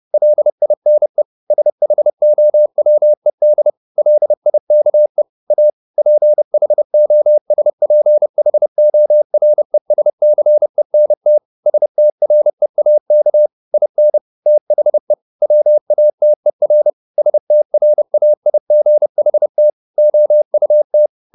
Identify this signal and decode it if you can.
Morse code, heard as a keyed tone, about 30 wpm, 600 Hz. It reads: LINE SHOWED LIKE A PHOSPHORESCENT STREAK IN THE WATER STRAIGHT OUT
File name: the old man and the sea / the old man and the sea 1 30wpm